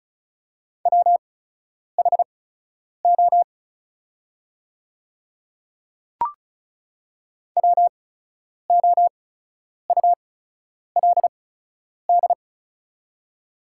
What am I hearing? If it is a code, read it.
Morse code, heard as a keyed tone, about 35 wpm, spaced as for 10 wpm, 700 Hz. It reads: WHO WOULD
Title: 8x Character Spacing - Top 100 Words - 35wpm